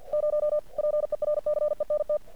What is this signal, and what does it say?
Morse code, heard as a keyed tone, about 50 wpm, 600 Hz. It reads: 0 PERCENT
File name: archived / CRSnet